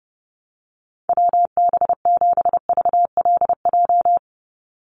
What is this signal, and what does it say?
Morse code, heard as a keyed tone, about 30 wpm, 700 Hz. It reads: WB7VLJ
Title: USA Call Signs 30wpm